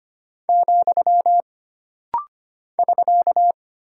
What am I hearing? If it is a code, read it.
Morse code, heard as a keyed tone, about 25 wpm, 700 Hz. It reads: , <SK>
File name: Single Punctuation-Prosign - Rapid-Fire - 25wpm